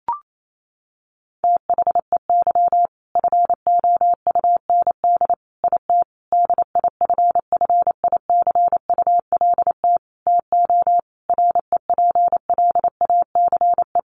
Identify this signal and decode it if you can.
Morse code, heard as a keyed tone, about 28 wpm, 700 Hz. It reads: THEY FOUND IT DIFFICULT TO REPLACE
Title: Sentences from Top 1400 Words 28wpm